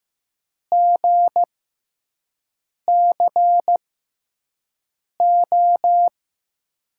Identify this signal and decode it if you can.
Morse code, heard as a keyed tone, about 15 wpm, spaced as for 5 wpm, 700 Hz. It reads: GCO
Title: USA Call Signs - 6x Character Spacing - 15wpm